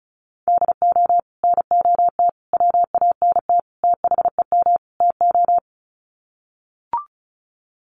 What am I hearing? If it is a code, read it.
Morse code, heard as a keyed tone, about 34 wpm, 700 Hz. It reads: DO NOT WANT THEM TO
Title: Sets of 5 Words Encore 35wpm